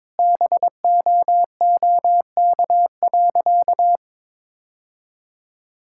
Morse code, heard as a keyed tone, about 22 wpm, 700 Hz. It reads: BOOK.
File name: practice / 22wpm / Sentences with 3-Word Phrases - 22wpm